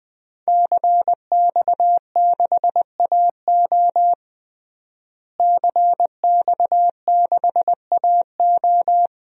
Morse code, heard as a keyed tone, about 20 wpm, 700 Hz. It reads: CX6AO CX6AO
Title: DXCC-100 Call Signs - 5 Characters - Repeated 3x - 20wpm